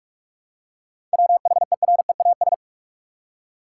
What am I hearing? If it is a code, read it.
Morse code, heard as a keyed tone, about 46 wpm, 700 Hz. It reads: WHEREAS
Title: Seven Letter Words - 45wpm